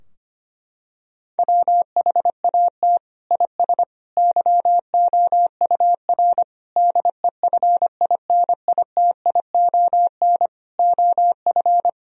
Morse code, heard as a keyed tone, about 25 wpm, 700 Hz. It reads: WHAT IS YOUR DEFINITION OF
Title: Sentences from Top 1000 Words - Review Part 4 - 25wpm